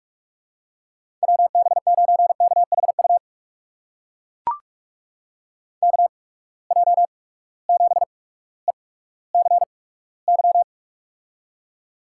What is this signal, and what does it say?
Morse code, heard as a keyed tone, about 42 wpm, spaced as for 19 wpm, 700 Hz. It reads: WB9KHU KJ7ECY
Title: USA Call Signs - 6 Characters - 8x Character Spacing - 45wpm